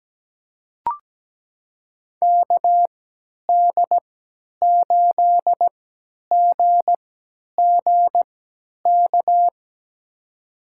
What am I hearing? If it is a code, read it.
Morse code, heard as a keyed tone, about 17 wpm, spaced as for 10 wpm, 700 Hz. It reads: KD8GGK
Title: USA Call Signs - 6 Characters - 3x Character Spacing - 17wpm